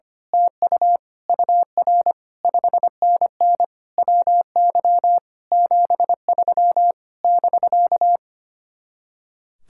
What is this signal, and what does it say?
Morse code, heard as a keyed tone, about 24 wpm, 700 Hz. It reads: TU UR 5NN WY 73 <BK>
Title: Speed-Racing Common QSO Phrases Encore - 25wpm